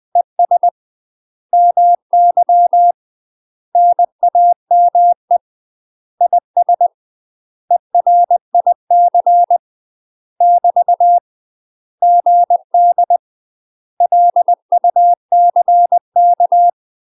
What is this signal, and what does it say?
Morse code, heard as a keyed tone, about 20 wpm, 700 Hz. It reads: ES MY NAME IS ERIC <BT> GD LUCK
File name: UK MASTER QSOs TRAINING / 20WPM / UK MASTER QSOs 0011